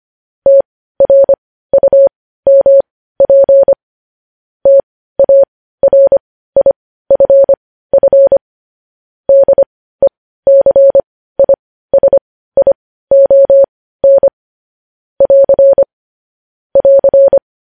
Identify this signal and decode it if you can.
Morse code, heard as a keyed tone, about 25 wpm, spaced as for 15 wpm, 550 Hz. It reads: TRUMP TARIFF DECISION <AR> <AR>
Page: news headlines in morse code at 15 wpm